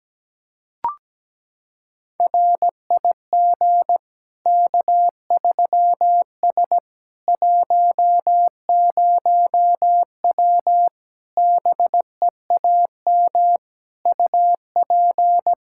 Morse code, heard as a keyed tone, about 17 wpm, 700 Hz. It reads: RIG K3S 10W BEAM UP
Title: Practice QSO - Exchange 2 - Info - 17wpm